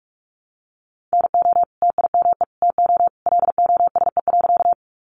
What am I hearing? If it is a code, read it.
Morse code, heard as a keyed tone, about 44 wpm, 700 Hz. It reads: NO TIME TO LOSE.
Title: Sentences from Top 1100 Words 45wpm